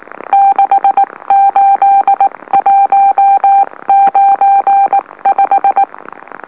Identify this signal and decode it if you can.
Morse code, heard as a keyed tone, about 19 wpm, spaced as for 16 wpm, 790 Hz. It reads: 68195
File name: M04- -2345utc-LOLO